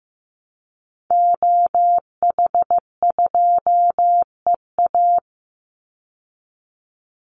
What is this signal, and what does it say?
Morse code, heard as a keyed tone, about 15 wpm, 700 Hz. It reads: OH2EA
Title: Call Signs Repeated 3x - 15wpm